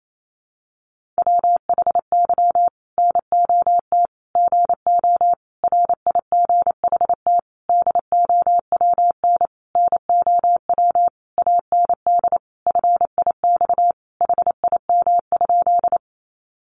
Morse code, heard as a keyed tone, about 28 wpm, 700 Hz. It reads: WHY NOT GO RIGHT DOWN NOW AND FIX HIM?
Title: Sentences from Top 1400 Words 28wpm